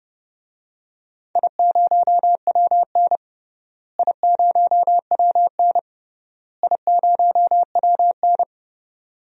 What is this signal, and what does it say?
Morse code, heard as a keyed tone, about 30 wpm, 700 Hz. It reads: I0WN I0WN I0WN
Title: DXCC-100 Call Signs - 4 Characters - Repeated 3x - 30wpm